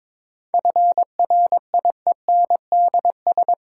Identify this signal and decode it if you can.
Morse code, heard as a keyed tone, about 22 wpm, 700 Hz. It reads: FRIENDS
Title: Sentences from Top 1000 Words - Review Part 5 - New Word Drill - 22wpm